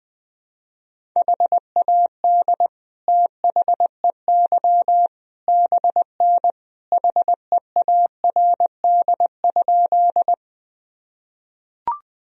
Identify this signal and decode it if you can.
Morse code, heard as a keyed tone, about 20 wpm, 700 Hz. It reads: HAD THEY BN HEARD?
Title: Sentences from Top 400 Words and Abbreviations 20wpm